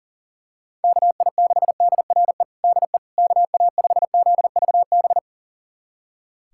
Morse code, heard as a keyed tone, about 40 wpm, 700 Hz. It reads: KI6DRE DE KA5ZVB
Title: USA Callsign DE Callsign 40wpm